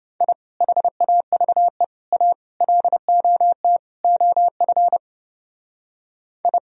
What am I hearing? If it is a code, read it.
Morse code, heard as a keyed tone, about 28 wpm, 700 Hz. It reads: I HAVE A LOT OF I